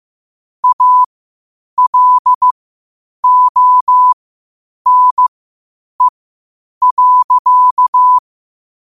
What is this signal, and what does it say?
Morse code, heard as a keyed tone, about 15 wpm, spaced as for 8 wpm, 1000 Hz. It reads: ALONE.
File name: qotdfiles / 8wpm